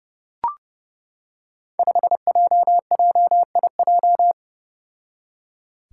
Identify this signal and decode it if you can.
Morse code, heard as a keyed tone, about 31 wpm, 700 Hz. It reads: 5JJIJ
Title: Five Challenging Letter-Number Combinations 30wpm